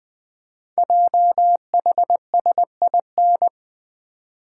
Morse code, heard as a keyed tone, about 20 wpm, 700 Hz. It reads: JHSIN
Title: Code Groups - 5 Letters - 20wpm